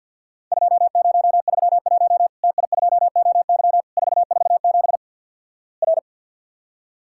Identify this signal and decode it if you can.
Morse code, heard as a keyed tone, about 50 wpm, 700 Hz. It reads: 2031 TI2OY 447 R
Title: Worked All Europe DX Contest - QTC Exchange - 50wpm